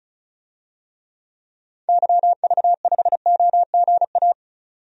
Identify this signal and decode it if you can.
Morse code, heard as a keyed tone, about 36 wpm, 700 Hz. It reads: YV5OGA
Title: DXCC-100 Call Signs - 6 Characters - 35wpm